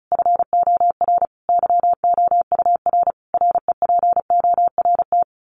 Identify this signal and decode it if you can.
Morse code, heard as a keyed tone, about 34 wpm, 700 Hz. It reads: FOR YOUR REPORT
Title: Common QSO Phrases 35wpm